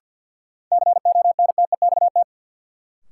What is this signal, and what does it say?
Morse code, heard as a keyed tone, about 50 wpm, 700 Hz. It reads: CONTEXT
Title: Sentences from Top 1000 Words - Review Part 7 - New Word Drill - 50wpm